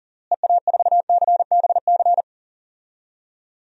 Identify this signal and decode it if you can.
Morse code, heard as a keyed tone, about 40 wpm, 700 Hz. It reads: EA4CBC
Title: DXCC-100 Call Signs - 6 Characters - Repeated 2x - 40wpm